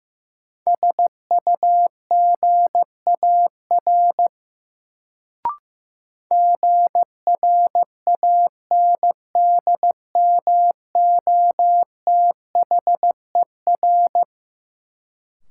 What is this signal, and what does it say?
Morse code, heard as a keyed tone, about 15 wpm, 700 Hz. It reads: SUGAR GRANDMOTHER
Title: Sentences from Top 1200 Words - New Word Drill - 15wpm